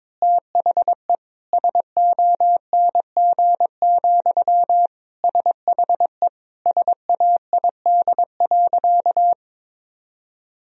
Morse code, heard as a keyed tone, about 22 wpm, 700 Hz. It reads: THE SONG, SHE SAID.
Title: Sentences from Top 1000 Words - Review Part 2 - 22wpm